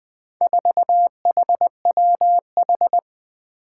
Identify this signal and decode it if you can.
Morse code, heard as a keyed tone, about 20 wpm, 700 Hz. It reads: 4HWH
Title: Four Challenging Letter-Number Combinations 20wpm